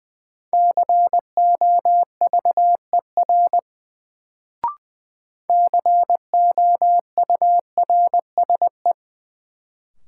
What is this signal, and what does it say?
Morse code, heard as a keyed tone, about 20 wpm, 700 Hz. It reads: COVER COURSE
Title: Sentences from Top 500 Words - New Word Drill - 20wpm